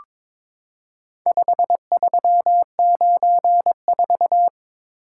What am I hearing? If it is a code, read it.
Morse code, heard as a keyed tone, about 22 wpm, 700 Hz. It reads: 5394
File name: practice / 22wpm / Four Letter-Number Combinations - 22wpm